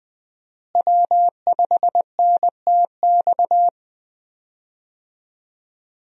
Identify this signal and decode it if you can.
Morse code, heard as a keyed tone, about 20 wpm, 700 Hz. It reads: W5NTX
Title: USA Call Signs - 5 Characters - Repeated 2x - 20wpm